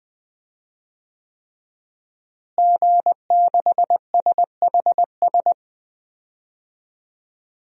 Morse code, heard as a keyed tone, about 21 wpm, 700 Hz. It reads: G6SHS
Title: DXCC-100 Call Signs - 5 Characters - Repeated 3x - 20wpm